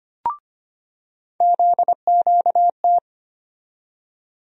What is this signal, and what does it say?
Morse code, heard as a keyed tone, about 25 wpm, 700 Hz. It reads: ZQT
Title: USA Call Sign Postfixes - 3 Letters 25wpm